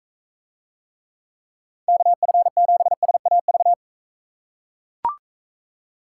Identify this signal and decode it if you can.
Morse code, heard as a keyed tone, about 42 wpm, 700 Hz. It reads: KF7SAV